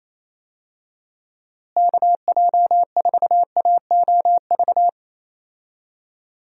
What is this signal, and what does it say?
Morse code, heard as a keyed tone, about 28 wpm, 700 Hz. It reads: KJ4AOV
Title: USA Call Signs - 6 Characters - Repeated 2x - 28wpm